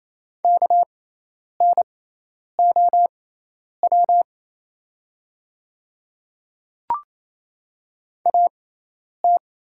Morse code, heard as a keyed tone, about 27 wpm, spaced as for 10 wpm, 700 Hz. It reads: KNOW AT
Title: 6x Character Spacing - Top 100 Words - 28wpm